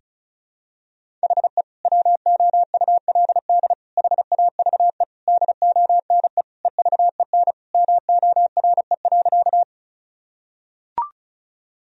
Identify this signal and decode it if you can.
Morse code, heard as a keyed tone, about 36 wpm, 700 Hz. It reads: HE WOULD HAVE DONE EVEN MORE.
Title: Sentences from Top 1000 Words - Review Part 1 - 35wpm